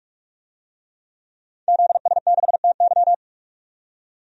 Morse code, heard as a keyed tone, about 46 wpm, 700 Hz. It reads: ZS6TY
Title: DXCC-100 Call Signs - 5 Characters - 45wpm